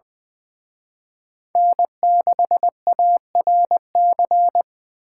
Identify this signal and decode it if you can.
Morse code, heard as a keyed tone, about 24 wpm, 700 Hz. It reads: N6ARC
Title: Speed Racing - USA 1x3 Call Signs 20wpm